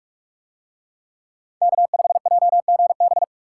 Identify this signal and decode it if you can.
Morse code, heard as a keyed tone, about 39 wpm, 700 Hz. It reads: K5JGB